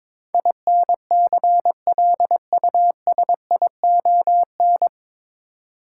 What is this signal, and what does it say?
Morse code, heard as a keyed tone, about 22 wpm, 700 Hz. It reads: INCLUSION
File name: 9 Letter Words - First 3 Letters Given - 22wpm